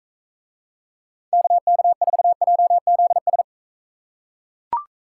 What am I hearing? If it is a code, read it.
Morse code, heard as a keyed tone, about 40 wpm, 700 Hz. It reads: KK4JZS